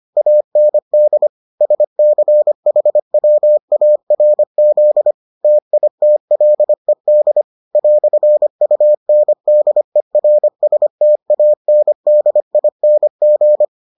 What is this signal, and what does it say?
Morse code, heard as a keyed tone, about 25 wpm, 600 Hz. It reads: AND SCHWARZ TITLED "UNDERSTANDING